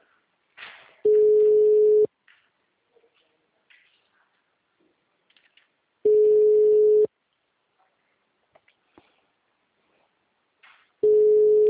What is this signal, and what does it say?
Ringback tone (Europe, 425 Hz).